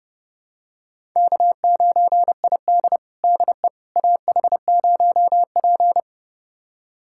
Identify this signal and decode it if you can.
Morse code, heard as a keyed tone, about 30 wpm, 700 Hz. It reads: K9ID DE AH0P
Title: USA Callsign DE Callsign - 4 Characters - 30wpm